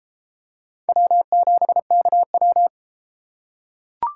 Morse code, heard as a keyed tone, about 29 wpm, 700 Hz. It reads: W7KW